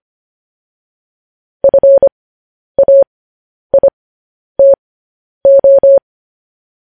Morse code, heard as a keyed tone, about 25 wpm, spaced as for 10 wpm, 550 Hz. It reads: FAITO
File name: podcast / News headlines-1x-25x10